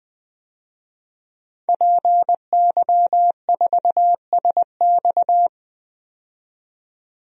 Morse code, heard as a keyed tone, about 20 wpm, 700 Hz. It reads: PY4SX
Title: DXCC-100 Call Signs - 5 Characters - 20wpm